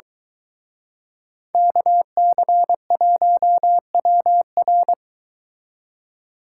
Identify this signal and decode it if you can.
Morse code, heard as a keyed tone, about 23 wpm, 700 Hz. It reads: KC1WR